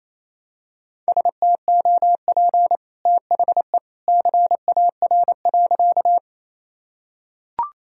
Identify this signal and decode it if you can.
Morse code, heard as a keyed tone, about 28 wpm, 700 Hz. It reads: STOP THE CAR.
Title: Sentences from Top 1000 Words - Review Part 1 - 28wpm